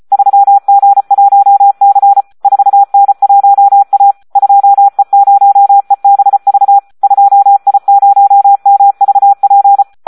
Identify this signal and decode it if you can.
Morse code, heard as a keyed tone, about 34 wpm, 800 Hz. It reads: 3G1C 4N1A 2E0EBV 2I0MVP
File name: Audio practica CW / SD 1 34